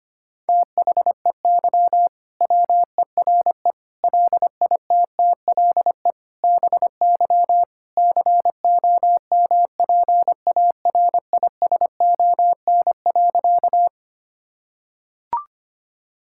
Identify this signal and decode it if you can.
Morse code, heard as a keyed tone, about 25 wpm, 700 Hz. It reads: THEY WERE LITTLE BY COMPARISON.
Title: Sentences from Top 1600 Words 25wpm